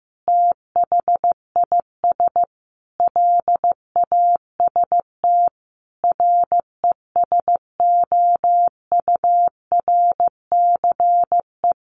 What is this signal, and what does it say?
Morse code, heard as a keyed tone, about 15 wpm, 700 Hz. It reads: THIS LAST RESOURCE
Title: Sentences from Top 1300 Words 15wpm